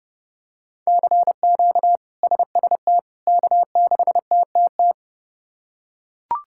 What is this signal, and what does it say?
Morse code, heard as a keyed tone, about 32 wpm, 700 Hz. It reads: CQ SST K6TTT
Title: K1USN SST QSO Practice - 30wpm